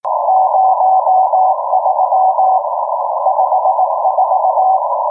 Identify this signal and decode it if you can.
Morse code, heard as a keyed tone, about 18 wpm, 750 Hz, with noise. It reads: 03 54